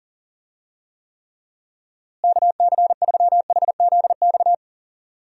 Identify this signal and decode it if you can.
Morse code, heard as a keyed tone, about 40 wpm, 700 Hz. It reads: KC3HZX
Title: USA Call Signs - 6 Characters - Repeated 3x - 40wpm